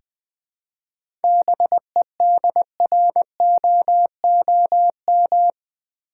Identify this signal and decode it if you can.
Morse code, heard as a keyed tone, about 20 wpm, 700 Hz. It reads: BEDROOM
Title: Seven Letter Words - 20wpm